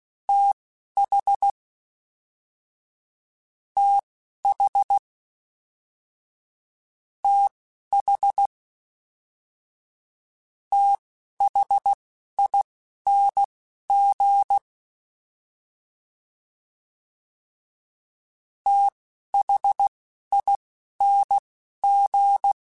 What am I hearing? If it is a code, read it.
Morse code, heard as a keyed tone, about 16 wpm, spaced as for 11 wpm, 800 Hz. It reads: TH TH TH THING THING
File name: TH / TH mp3s / TH Prefixes 3x2 Twice @10